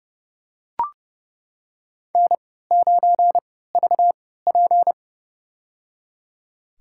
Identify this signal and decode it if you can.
Morse code, heard as a keyed tone, about 31 wpm, spaced as for 17 wpm, 700 Hz. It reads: N9VP
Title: USA Call Signs - 3x Character Spacing - 30wpm